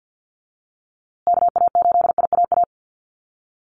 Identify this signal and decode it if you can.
Morse code, heard as a keyed tone, about 50 wpm, 700 Hz. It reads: KA8IAA
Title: USA Call Signs Repeated 3x 50wpm